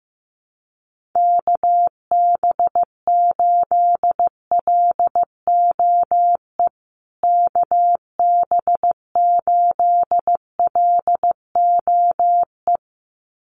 Morse code, heard as a keyed tone, about 15 wpm, 700 Hz. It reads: KB8LOE KB8LOE
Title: USA Call Signs Repeated 2x 15wpm